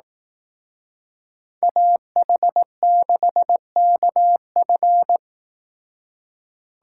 Morse code, heard as a keyed tone, about 18 wpm, 700 Hz. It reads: AH6KF
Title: Speed Racing - USA 2x2 Call Signs 15wpm